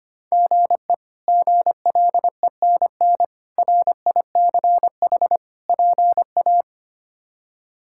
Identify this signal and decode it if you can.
Morse code, heard as a keyed tone, about 25 wpm, 700 Hz. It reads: GE GLENN RICH PA
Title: K1USN SST QSO Practice - 25wpm